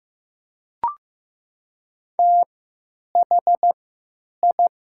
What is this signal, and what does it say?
Morse code, heard as a keyed tone, about 16 wpm, spaced as for 9 wpm, 700 Hz. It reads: THI
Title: 3x Character Spacing - Top 100 Words - 15wpm